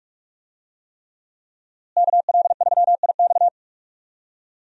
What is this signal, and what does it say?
Morse code, heard as a keyed tone, about 46 wpm, 700 Hz. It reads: KL3IX
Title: USA Call Signs - 5 Characters - Repeated 3x - 45wpm